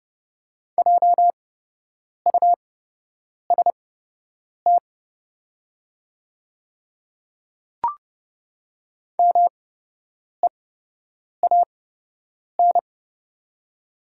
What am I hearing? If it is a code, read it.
Morse code, heard as a keyed tone, about 30 wpm, spaced as for 8 wpm, 700 Hz. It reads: JUST MEAN
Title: 8x Character Spacing - Top 200 Words - 30wpm